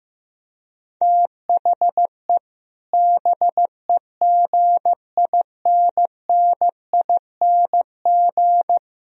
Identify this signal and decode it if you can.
Morse code, heard as a keyed tone, about 15 wpm, 700 Hz. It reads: THE BEGINNING
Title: Speed-Racing - Sets of 4 Words 15wpm